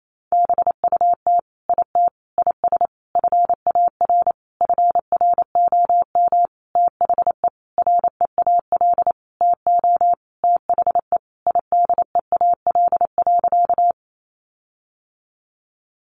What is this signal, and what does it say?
Morse code, heard as a keyed tone, about 28 wpm, 700 Hz. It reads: BUT IT IS FAR FROM THE REAL TO THE IDEAL.
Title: Sentences from Top 1400 Words 28wpm